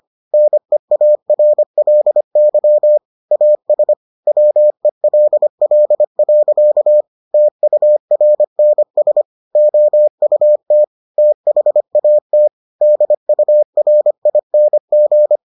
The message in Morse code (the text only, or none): NEARLY AS WELL. TURNS OUT THAT DURING